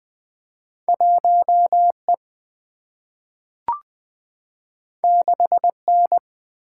Morse code, heard as a keyed tone, about 20 wpm, 700 Hz. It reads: 1E 6N
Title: Single-Letter Single-Number 20wpm